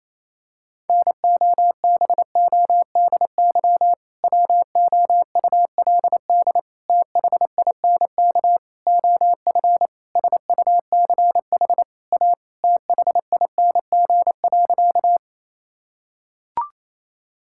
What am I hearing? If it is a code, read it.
Morse code, heard as a keyed tone, about 28 wpm, 700 Hz. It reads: NOBODY WOULD THINK OF SUCH A THING.
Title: Sentences from Top 1600 Words 28wpm